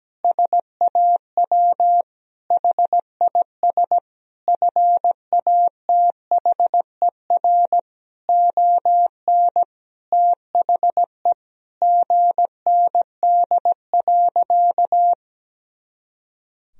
Morse code, heard as a keyed tone, about 17 wpm, 700 Hz. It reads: SAW HIS FATHER ON THE GND.
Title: Sentences from Top 500 Words and Abbreviations 17wpm